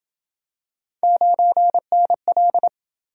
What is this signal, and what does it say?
Morse code, heard as a keyed tone, about 27 wpm, 700 Hz. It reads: 9NL